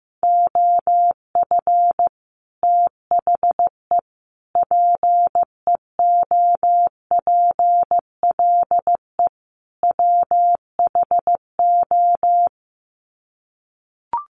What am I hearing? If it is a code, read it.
Morse code, heard as a keyed tone, about 14 wpm, 700 Hz. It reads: OF THE PEOPLE WHO
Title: Sets of 4 Words 15wpm